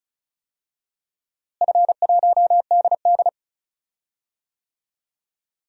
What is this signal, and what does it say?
Morse code, heard as a keyed tone, about 36 wpm, 700 Hz. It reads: F1DD
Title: DXCC-100 Call Signs - 4 Characters - Repeated 2x - 35wpm